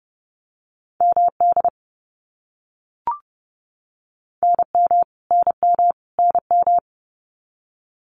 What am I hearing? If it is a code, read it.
Morse code, heard as a keyed tone, about 30 wpm, 700 Hz. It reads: MD NM NM NM
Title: US States Repeated 3x 30wpm